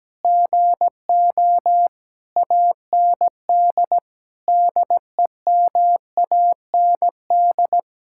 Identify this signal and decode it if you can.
Morse code, heard as a keyed tone, about 17 wpm, 700 Hz. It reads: GO AND DEMAND
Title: Sentences from Top 1600 Words 17wpm